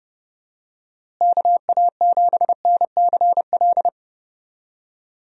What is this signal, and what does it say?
Morse code, heard as a keyed tone, about 31 wpm, 700 Hz. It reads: KA7NCL
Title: USA Call Signs - 6 Characters - Repeated 3x - 30wpm